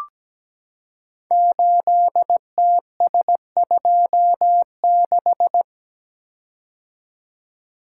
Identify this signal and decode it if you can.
Morse code, heard as a keyed tone, about 17 wpm, 700 Hz. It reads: 8TS26
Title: Code Groups - 5 Letter:Numbers - 17wpm